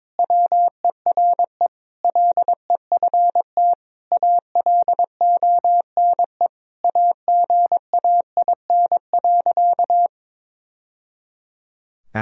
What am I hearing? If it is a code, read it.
Morse code, heard as a keyed tone, about 22 wpm, 700 Hz. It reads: WERE LEFT ALONE AGAIN.
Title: Sentences from Top 1000 Words - Review Part 1 - 22wpm